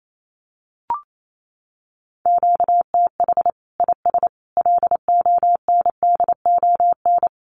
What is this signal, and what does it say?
Morse code, heard as a keyed tone, about 28 wpm, 700 Hz. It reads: QTH IS LONDON
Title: Common QSO Phrases 28wpm